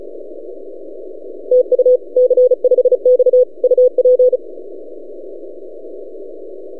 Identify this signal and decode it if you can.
Morse code, heard as a keyed tone, about 35 wpm, 500 Hz. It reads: TU C5X UP
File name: posts / c5x-30cw- -0600z